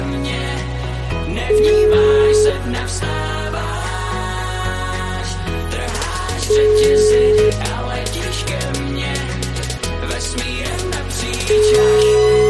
Ringback tone (Europe, 425 Hz), with noise.